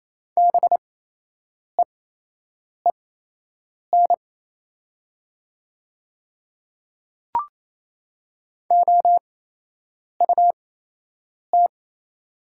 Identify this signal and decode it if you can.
Morse code, heard as a keyed tone, about 28 wpm, spaced as for 8 wpm, 700 Hz. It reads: BEEN OUT
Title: 8x Character Spacing - Top 200 Words - 28wpm